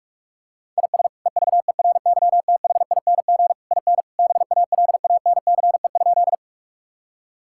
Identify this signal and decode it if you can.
Morse code, heard as a keyed tone, about 41 wpm, 700 Hz. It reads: IS EVERYTHING IN BALANCE?
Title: Sentences from Top 1000 Words - Review Part 6 - 45wpm